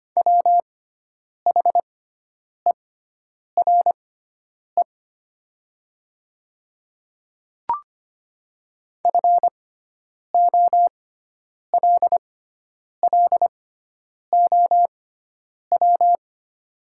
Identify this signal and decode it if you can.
Morse code, heard as a keyed tone, about 25 wpm, spaced as for 9 wpm, 700 Hz. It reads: WHERE FOLLOW
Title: 6x Character Spacing - Top 300 Words - 25wpm